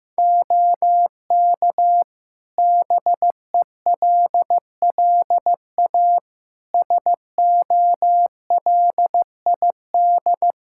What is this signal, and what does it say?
Morse code, heard as a keyed tone, about 16 wpm, 700 Hz. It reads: OK BELLA SOLID
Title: K1USN SST QSO Practice 15wpm